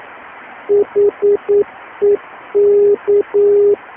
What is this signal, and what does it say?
Morse code, heard as a keyed tone, about 9 wpm, 410 Hz. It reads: HEK